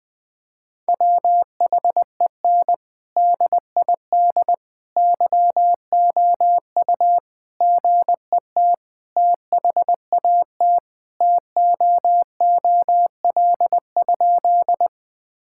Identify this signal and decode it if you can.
Morse code, heard as a keyed tone, about 21 wpm, 700 Hz. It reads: WHEN DID YOU GET THAT TOOL?
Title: Sentences from Top 1000 Words - Review Part 6 - 20wpm